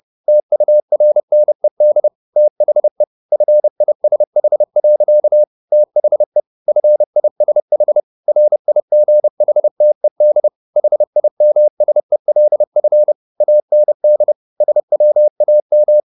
Morse code, heard as a keyed tone, about 30 wpm, 600 Hz. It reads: TURNED THE FISH. THE FISH RIGHTED HIMSELF AND SWAM